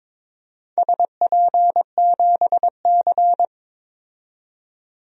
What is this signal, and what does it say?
Morse code, heard as a keyed tone, about 21 wpm, 700 Hz. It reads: SP7C
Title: DXCC-100 Call Signs - 4 Characters - 22wpm